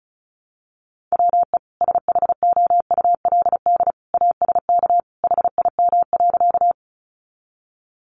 Morse code, heard as a keyed tone, about 34 wpm, 700 Hz. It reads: WE SHOULD ASK HIM.
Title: Sentences from Top 300 Words 35wpm